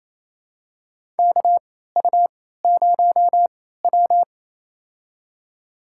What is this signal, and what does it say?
Morse code, heard as a keyed tone, about 28 wpm, spaced as for 16 wpm, 700 Hz. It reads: KU0W
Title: USA Call Signs - 4 Characters - 3x Character Spacing - 28wpm